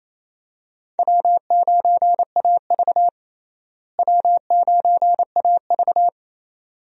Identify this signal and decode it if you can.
Morse code, heard as a keyed tone, about 28 wpm, 700 Hz. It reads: W9AV W9AV
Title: USA Call Signs - 4 Characters - Repeated 2x - 28wpm